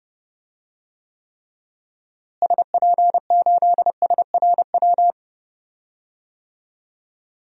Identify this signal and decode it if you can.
Morse code, heard as a keyed tone, about 30 wpm, 700 Hz. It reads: SP8SRW
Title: DXCC-100 Call Signs - 6 Characters - Repeated 3x - 30wpm